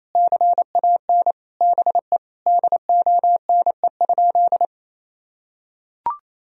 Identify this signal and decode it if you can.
Morse code, heard as a keyed tone, about 28 wpm, 700 Hz. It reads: CAN BE DONE?
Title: Sentences from Top 1000 Words - Review Part 1 - 28wpm